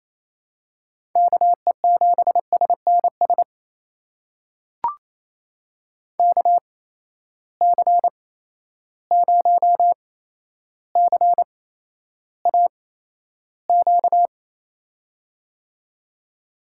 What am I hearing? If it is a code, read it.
Morse code, heard as a keyed tone, about 28 wpm, spaced as for 12 wpm, 700 Hz. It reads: KE7SNS KC0CAQ
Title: USA Call Signs - 6 Characters - 8x Character Spacing - 28wpm